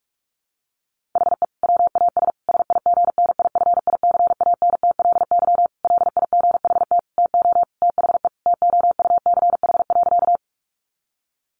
Morse code, heard as a keyed tone, about 44 wpm, 700 Hz. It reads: HE WAS SIGNIFICANTLY LIGHT TO THE TOUCH.
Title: Sentences from Top 1500 Words 45wpm